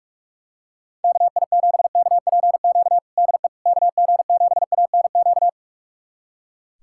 Morse code, heard as a keyed tone, about 42 wpm, 700 Hz. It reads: KI7KPQ DE KG7ANQ
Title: USA Callsign DE Callsign - 6 Characters - 45wpm